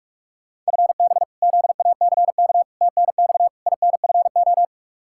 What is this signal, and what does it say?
Morse code, heard as a keyed tone, about 42 wpm, 700 Hz. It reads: FB ZACK TNX INFO
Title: Practice QSO - Exchange 3 - More Info - 45wpm